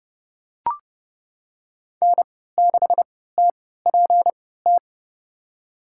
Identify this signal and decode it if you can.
Morse code, heard as a keyed tone, about 30 wpm, spaced as for 17 wpm, 700 Hz. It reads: N6TPT
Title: USA Call Signs - 5 Characters - 3x Character Spacing - 30wpm